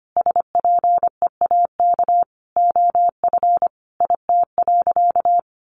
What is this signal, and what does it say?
Morse code, heard as a keyed tone, about 25 wpm, 700 Hz. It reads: SPEAK OF IT.
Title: Sentences from Top 1500 Words 25wpm